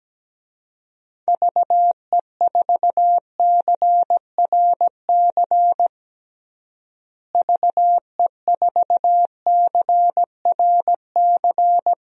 Morse code, heard as a keyed tone, about 17 wpm, 700 Hz. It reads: VE4CRC VE4CRC
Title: DXCC-100 Call Signs - 6 Characters - Repeated 2x - 17wpm